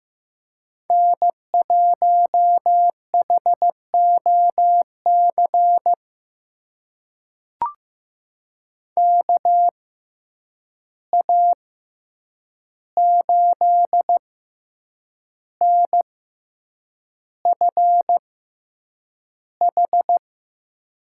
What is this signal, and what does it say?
Morse code, heard as a keyed tone, about 15 wpm, spaced as for 7 wpm, 700 Hz. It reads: N1HOC KA8NFH
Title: USA Call Signs - 6x Character Spacing - 15wpm